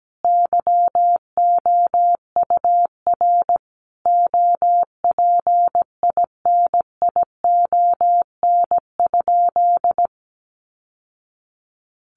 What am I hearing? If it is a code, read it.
Morse code, heard as a keyed tone, about 17 wpm, 700 Hz. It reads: YOUR OPINION?
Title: Sentences from Top 1200 Words 17wpm